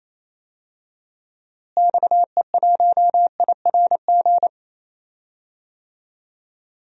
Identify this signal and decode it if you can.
Morse code, heard as a keyed tone, about 28 wpm, 700 Hz. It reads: XE1IRG
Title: DXCC-100 Call Signs - 6 Characters - Repeated 2x - 28wpm